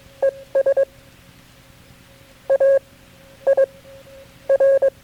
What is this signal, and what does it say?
Morse code, heard as a keyed tone, about 22 wpm, spaced as for 11 wpm, 540 Hz. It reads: ES AIR
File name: KB6MT-tape3-side1